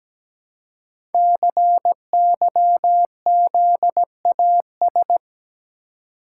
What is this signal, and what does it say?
Morse code, heard as a keyed tone, about 17 wpm, 700 Hz. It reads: CYZAS
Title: Code Groups - 5 Letters - 17wpm